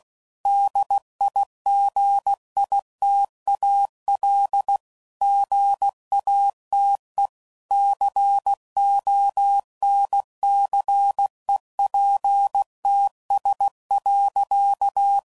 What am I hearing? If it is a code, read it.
Morse code, heard as a keyed tone, about 16 wpm, 800 Hz. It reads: DIGITAL GATE CONCEPTS.
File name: CW Code Groups @15WPM / Technical Subjects / Gates Intro